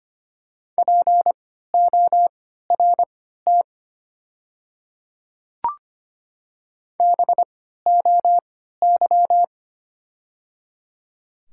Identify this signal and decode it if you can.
Morse code, heard as a keyed tone, about 25 wpm, spaced as for 14 wpm, 700 Hz. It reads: PORT BOY
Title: 3x Character Spacing - Top 300 Words - 25wpm